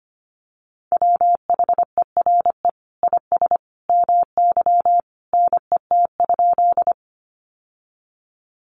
Morse code, heard as a keyed tone, about 25 wpm, 700 Hz. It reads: WHERE IS MY NET?
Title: Sentences from Top 1400 Words 25wpm